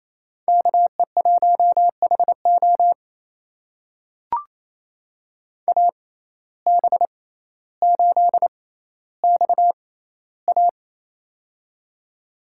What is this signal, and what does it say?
Morse code, heard as a keyed tone, about 28 wpm, spaced as for 14 wpm, 700 Hz. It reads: KE1HO AB8XA
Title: USA Call Signs - 5 Characters - 6x Character Spacing - 28wpm